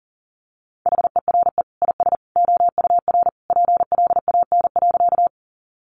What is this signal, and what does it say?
Morse code, heard as a keyed tone, about 40 wpm, 700 Hz. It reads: HERE IS OUR PLAN.
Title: Sentences from Top 500 Words 40wpm